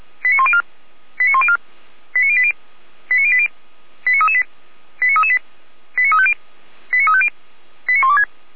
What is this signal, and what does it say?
Five-tone selective call (ZVEI), heard as quick runs of five tones, tones of about 70 ms. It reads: 89104 89104 89080 89080 89338 89338 89350 89350 89126
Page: ZVEI